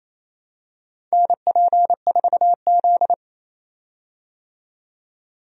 Morse code, heard as a keyed tone, about 28 wpm, 700 Hz. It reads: NP4Z
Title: USA Call Signs - 4 Characters - 28wpm